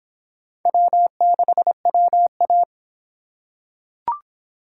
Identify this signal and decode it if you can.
Morse code, heard as a keyed tone, about 24 wpm, 700 Hz. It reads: W6WA